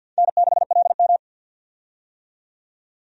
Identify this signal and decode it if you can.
Morse code, heard as a keyed tone, about 53 wpm, 700 Hz. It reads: N6LM
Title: USA Call Signs - 4 Characters - 50wpm